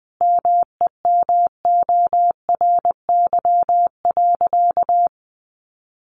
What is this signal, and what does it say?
Morse code, heard as a keyed tone, about 20 wpm, 700 Hz. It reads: MEMORY.
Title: Sentences from Top 1300 Words 20wpm